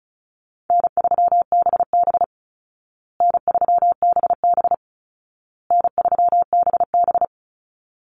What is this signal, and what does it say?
Morse code, heard as a keyed tone, about 34 wpm, 700 Hz. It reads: N3BB N3BB N3BB
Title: Easy Call Signs Repeated 3x - 35wpm